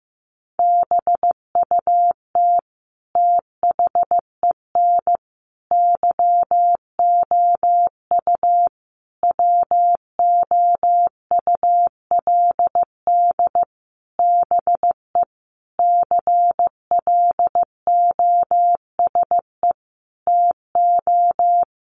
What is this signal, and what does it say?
Morse code, heard as a keyed tone, about 15 wpm, 700 Hz. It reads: BUT THEN YOU WOULD BE CLOSE TO
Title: Sentences from Top 300 Words 15wpm